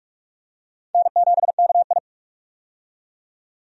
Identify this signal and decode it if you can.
Morse code, heard as a keyed tone, about 46 wpm, 700 Hz. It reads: N7KI
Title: USA Call Signs - 4 Characters - 45wpm